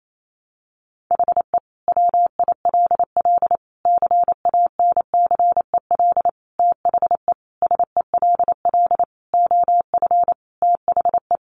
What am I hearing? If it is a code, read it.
Morse code, heard as a keyed tone, about 28 wpm, 700 Hz. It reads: HE WILL CANCEL THE SELL OF THE
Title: Sentences from Top 1200 Words 28wpm